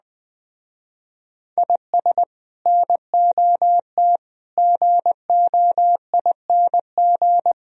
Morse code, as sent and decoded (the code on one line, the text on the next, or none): .. ... / -. --- - / --. --- .. -. --.
IS NOT GOING